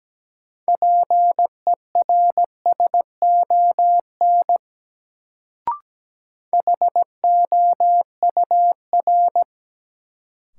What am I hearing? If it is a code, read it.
Morse code, heard as a keyed tone, about 17 wpm, 700 Hz. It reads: PERSON HOUR
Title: Sentences from Top 400 Words - New Word Drill - 17wpm